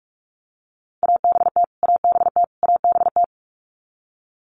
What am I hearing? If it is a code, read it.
Morse code, heard as a keyed tone, about 44 wpm, 700 Hz. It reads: ABT ABT ABT
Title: Abbreviations Repeated 3x - 45wpm